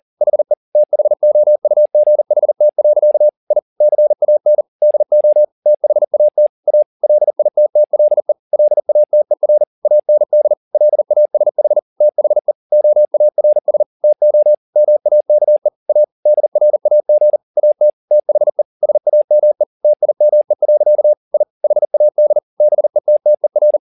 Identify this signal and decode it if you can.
Morse code, heard as a keyed tone, about 40 wpm, 600 Hz. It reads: HE THOUGHT. I CAN DO THAT A LITTLE LATER AND LASH THE OARS TO MAKE A DRAG AT THE SAME TIME. I HAD BETTER
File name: the old man and the sea 2 40wpm